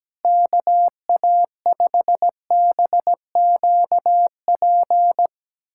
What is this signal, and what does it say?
Morse code, heard as a keyed tone, about 17 wpm, 700 Hz. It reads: KA5BQP
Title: USA Callsign DE Callsign - 6 Characters - 17wpm